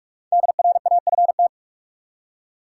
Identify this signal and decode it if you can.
Morse code, heard as a keyed tone, about 46 wpm, 700 Hz. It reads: DRAFT
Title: Five Letter Words - 45wpm